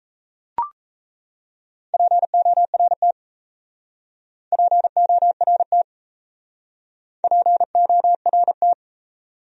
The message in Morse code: PORT PORT PORT

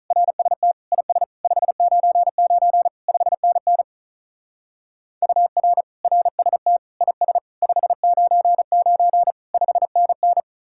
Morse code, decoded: RST IS 599 5NN UR RST IS 599 5NN

